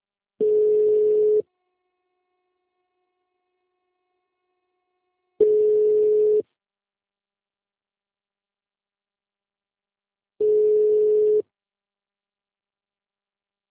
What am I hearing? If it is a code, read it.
Ringback tone (Europe, 425 Hz).
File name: out- -2006- - - .362